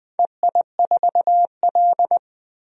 Morse code, heard as a keyed tone, about 20 wpm, 700 Hz. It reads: EI4L
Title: DXCC-100 Call Signs - 4 Characters - 20wpm